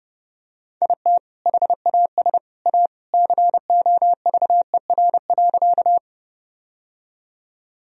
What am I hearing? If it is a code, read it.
Morse code, heard as a keyed tone, about 30 wpm, 700 Hz. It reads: IT HAS A COVER.
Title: Sentences from Top 1000 Words - Review Part 1 - 30wpm